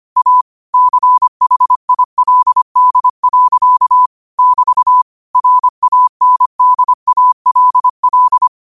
Morse code, heard as a keyed tone, about 25 wpm, 1000 Hz. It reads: A CHILD. <BT> RANDALL
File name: qotdfiles / 25wpm